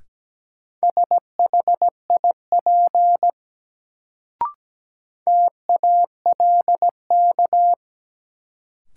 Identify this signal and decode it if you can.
Morse code, heard as a keyed tone, about 17 wpm, 700 Hz. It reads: SHIP TALK
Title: Sentences from Top 400 Words - New Word Drill - 17wpm